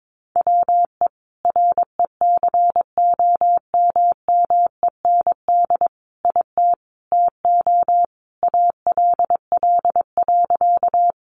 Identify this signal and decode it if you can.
Morse code, heard as a keyed tone, about 22 wpm, 700 Hz. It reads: WE RECOMMEND IT TO ALL.
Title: Sentences from Top 1200 Words 22wpm